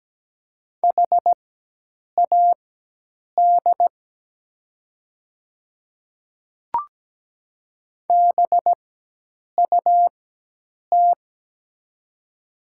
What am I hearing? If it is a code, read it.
Morse code, heard as a keyed tone, about 17 wpm, spaced as for 8 wpm, 700 Hz. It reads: HAD BUT
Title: 4x Character Spacing - Top 100 Words - 17wpm